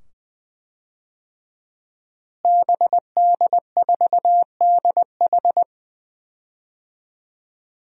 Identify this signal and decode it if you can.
Morse code, heard as a keyed tone, about 20 wpm, 700 Hz. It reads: BD4DH
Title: DXCC-100 Call Signs - 5 Characters - Repeated 2x - 20wpm